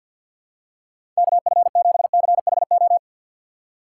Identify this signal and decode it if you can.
Morse code, heard as a keyed tone, about 52 wpm, 700 Hz. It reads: KF7CHO
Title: USA Call Signs - 6 Characters - 3x Character Spacing - 50wpm